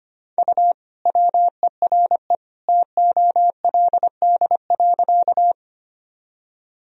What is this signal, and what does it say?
Morse code, heard as a keyed tone, about 25 wpm, 700 Hz. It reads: U WERE TOLD.
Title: Sentences from Top 400 Words and Abbreviations 25wpm